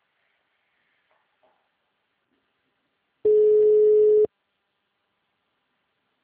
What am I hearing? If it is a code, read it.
Ringback tone (Europe, 425 Hz).
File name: out- -2031- - - .14401